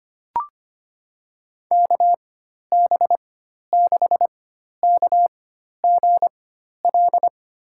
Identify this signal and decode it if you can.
Morse code, heard as a keyed tone, about 25 wpm, spaced as for 12 wpm, 700 Hz. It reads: KB6KGL
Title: USA Call Signs - 6 Characters - 4x Character Spacing - 25wpm